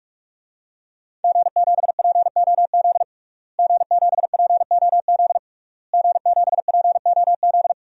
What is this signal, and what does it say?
Morse code, heard as a keyed tone, about 46 wpm, 700 Hz. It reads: G7POZ G7POZ G7POZ
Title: DXCC-100 Call Signs - 5 Characters - Repeated 3x - 45wpm